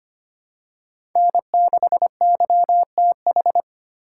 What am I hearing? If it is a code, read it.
Morse code, heard as a keyed tone, about 25 wpm, 700 Hz. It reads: N6YTH
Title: USA Call Signs - 5 Characters - 2x Character Spacing - 25wpm